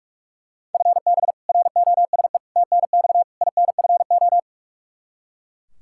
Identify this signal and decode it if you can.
Morse code, heard as a keyed tone, about 42 wpm, 700 Hz. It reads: FB ROSE TNX INFO
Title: Practice QSO - Exchange 3 - More Info - 45wpm